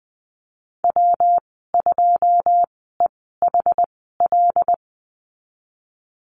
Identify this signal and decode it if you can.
Morse code, heard as a keyed tone, about 20 wpm, spaced as for 14 wpm, 700 Hz. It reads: W2EHL
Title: 128 Lesson 21g - Number 6 - Call Signs 20wpm